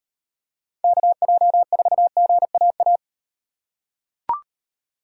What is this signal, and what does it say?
Morse code, heard as a keyed tone, about 38 wpm, 700 Hz. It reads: KJ4GAA